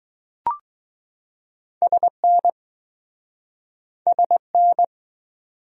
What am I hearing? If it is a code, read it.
Morse code, heard as a keyed tone, about 21 wpm, 700 Hz. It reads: SN SN